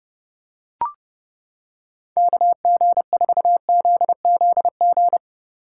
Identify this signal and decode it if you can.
Morse code, heard as a keyed tone, about 29 wpm, 700 Hz. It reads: KG4ZZG